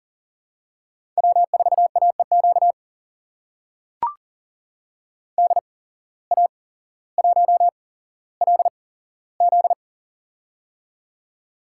Morse code, heard as a keyed tone, about 40 wpm, spaced as for 17 wpm, 700 Hz. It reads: W4AEQ DA1LZ
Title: USA Call Signs - 8x Character Spacing - 40wpm